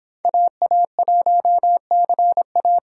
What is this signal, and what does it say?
Morse code, heard as a keyed tone, about 26 wpm, 700 Hz. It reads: AA1CA